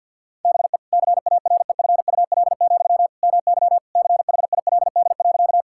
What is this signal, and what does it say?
Morse code, heard as a keyed tone, about 50 wpm, 700 Hz. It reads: BE CAREFUL, MY CHILD.
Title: Sentences from Top 1600 Words 50wpm